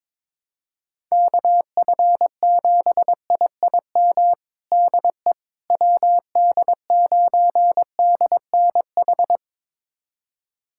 Morse code, heard as a keyed tone, about 21 wpm, 700 Hz. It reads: KF7IIM DE WD9DNH
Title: USA Callsign DE Callsign - 6 Characters - 22wpm